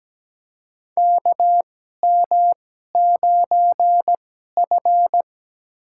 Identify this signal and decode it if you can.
Morse code, heard as a keyed tone, about 17 wpm, spaced as for 12 wpm, 700 Hz. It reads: KM9F
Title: USA Call Signs - 4 Characters - 2x Character Spacing - 17wpm